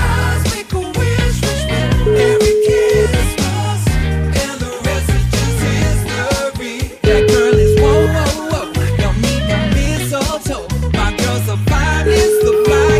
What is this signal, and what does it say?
Ringback tone (Europe, 425 Hz), with noise.